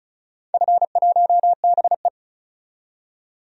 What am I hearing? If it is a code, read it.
Morse code, heard as a keyed tone, about 36 wpm, 700 Hz. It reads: F1BE
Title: DXCC-100 Call Signs - 4 Characters - 35wpm